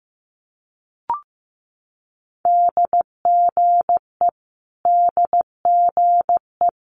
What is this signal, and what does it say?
Morse code, heard as a keyed tone, about 15 wpm, 700 Hz. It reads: DGE DGE
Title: Multi-letter Phonemes Repeated 3x 15wpm